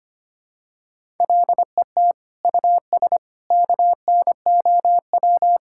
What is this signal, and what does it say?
Morse code, heard as a keyed tone, about 25 wpm, 700 Hz. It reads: LET US KNOW
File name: practice / 25wpm / Sentences with 9-Word Phrases - 25wpm